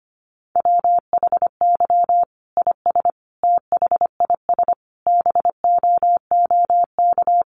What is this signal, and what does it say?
Morse code, heard as a keyed tone, about 25 wpm, 700 Hz. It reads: WHY IS THIS BOOK
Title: Sentences from Top 1200 Words 25wpm